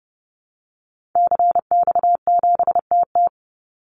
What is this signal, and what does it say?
Morse code, heard as a keyed tone, about 30 wpm, 700 Hz. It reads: CX7TT
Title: Call Signs Repeated 3x - 30wpm